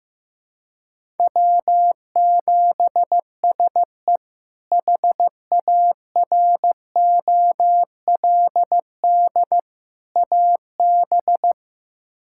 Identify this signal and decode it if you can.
Morse code, heard as a keyed tone, about 15 wpm, 700 Hz. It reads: W7SE HAROLD AB
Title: K1USN SST QSO Practice - 15wpm